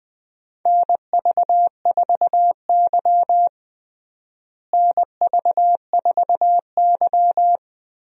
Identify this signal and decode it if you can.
Morse code, heard as a keyed tone, about 21 wpm, 700 Hz. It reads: NV4Y NV4Y
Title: USA Call Signs - 4 Characters - Repeated 3x - 20wpm